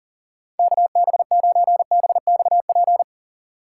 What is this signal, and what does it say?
Morse code, heard as a keyed tone, about 38 wpm, 700 Hz. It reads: KB9BXP
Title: USA Call Signs - 2x Character Spacing - 40wpm